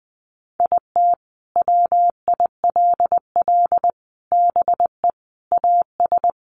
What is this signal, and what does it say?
Morse code, heard as a keyed tone, about 20 wpm, 700 Hz. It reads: IT WILL BE AS
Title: Sentences from Top 1400 Words 20wpm